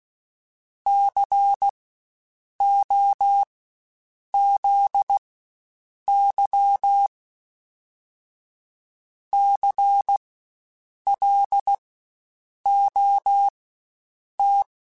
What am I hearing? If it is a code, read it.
Morse code, heard as a keyed tone, about 16 wpm, spaced as for 7 wpm, 800 Hz. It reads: COZY CLOT
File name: Spaced Alphas / Spaced Alphas mp3s / Cs sp @15